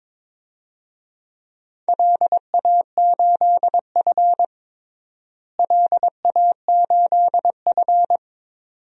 Morse code, heard as a keyed tone, about 22 wpm, 700 Hz. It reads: LA8F LA8F
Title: DXCC-100 Call Signs - 4 Characters - Repeated 2x - 22wpm